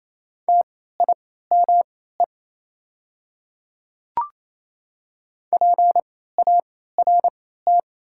Morse code, heard as a keyed tone, about 28 wpm, spaced as for 16 wpm, 690 Hz. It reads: TIME PART
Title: 3x Character Spacing - Top 500 Words - 28wpm